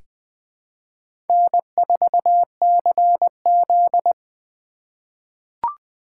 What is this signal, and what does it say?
Morse code, heard as a keyed tone, about 20 wpm, 700 Hz. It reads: N4CZ
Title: USA Call Signs - 4 Characters - 3x Character Spacing - 20wpm